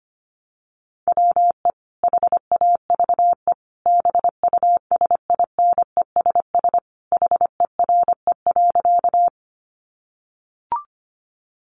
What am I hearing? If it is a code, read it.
Morse code, heard as a keyed tone, about 25 wpm, 700 Hz. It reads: WE HAVE BUSINESS HERE.
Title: Sentences from Top 600 Words 25wpm